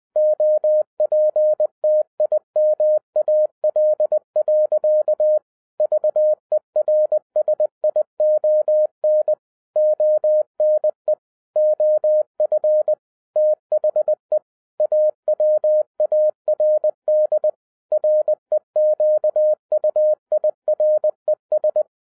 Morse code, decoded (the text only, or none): OPTIMAL. VERSION ONE OF THE AWARD REQUIRES